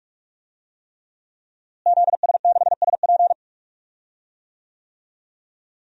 Morse code, heard as a keyed tone, about 44 wpm, 700 Hz. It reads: ZS6SP
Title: DXCC-100 Call Signs - 5 Characters - Repeated 3x - 45wpm